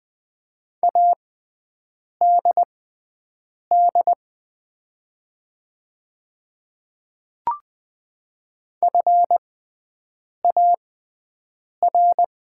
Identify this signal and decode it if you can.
Morse code, heard as a keyed tone, about 20 wpm, spaced as for 7 wpm, 700 Hz. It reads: ADD FAR
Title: 6x Character Spacing - Top 300 Words - 20wpm